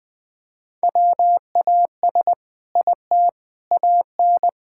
Morse code, heard as a keyed tone, about 20 wpm, 700 Hz. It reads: WAS IT AN
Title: Sentences from Top 1000 Words - Review Part 5 - 20wpm